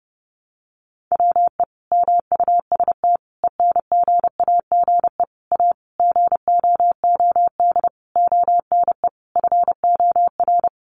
Morse code, heard as a keyed tone, about 30 wpm, 700 Hz. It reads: WE MUST ENGAGE A GOOD ONE FOR
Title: Sentences from Top 1500 Words 30wpm